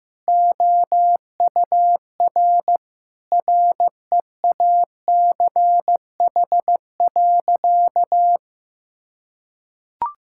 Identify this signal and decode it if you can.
Morse code, heard as a keyed tone, about 15 wpm, 700 Hz. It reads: OUR REACH.
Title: Sentences from Top 1000 Words - Review Part 6 - 15wpm